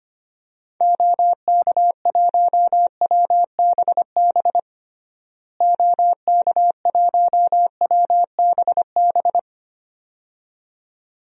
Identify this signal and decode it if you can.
Morse code, heard as a keyed tone, about 25 wpm, 700 Hz. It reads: OK1WBB OK1WBB
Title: DXCC-100 Call Signs - 6 Characters - Repeated 2x - 25wpm